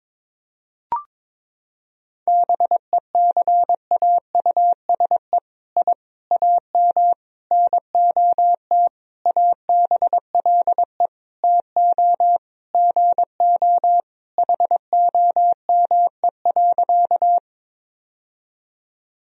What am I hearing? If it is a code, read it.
Morse code, heard as a keyed tone, about 21 wpm, 700 Hz. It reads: BECAUSE I AM NOT ABLE TO GO HOME.
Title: Sentences from Top 1000 Words - Review Part 3 - 22wpm